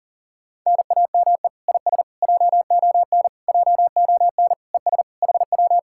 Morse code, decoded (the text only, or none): NAME IS JON JON ES HW